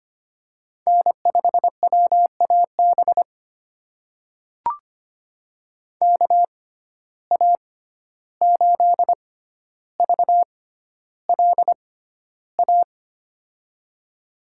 Morse code, heard as a keyed tone, about 25 wpm, spaced as for 12 wpm, 700 Hz. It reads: N5WAB KA8VLA
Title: USA Call Signs - 6x Character Spacing - 25wpm